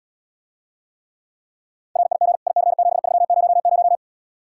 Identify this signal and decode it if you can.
Morse code, heard as a keyed tone, about 79 wpm, 700 Hz. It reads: RIG IC7300